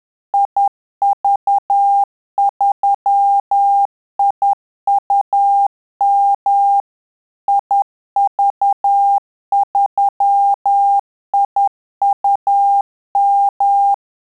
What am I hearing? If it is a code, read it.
Morse code, heard as a keyed tone, about 10 wpm, 800 Hz. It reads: IV3IUM IV3IUM
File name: Telegraphy-group / QSO CW 50 CPM